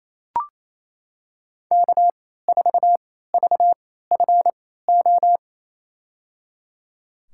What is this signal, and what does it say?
Morse code, heard as a keyed tone, about 28 wpm, spaced as for 16 wpm, 700 Hz. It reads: K4VFO
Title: USA Call Signs - 3x Character Spacing - 28wpm